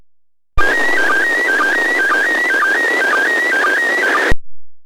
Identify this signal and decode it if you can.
SSTV (slow-scan television), PD 120.
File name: posts / receiving-sstv-images-from-iss / iss sstv2